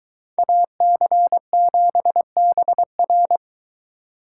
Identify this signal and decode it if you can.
Morse code, heard as a keyed tone, about 22 wpm, 700 Hz. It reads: AC7BR